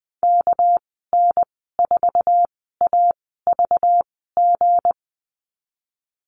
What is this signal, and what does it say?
Morse code, heard as a keyed tone, about 20 wpm, spaced as for 15 wpm, 700 Hz. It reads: KN4AVG
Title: 207 Lesson 33g - Letter j - Call Signs 20wpm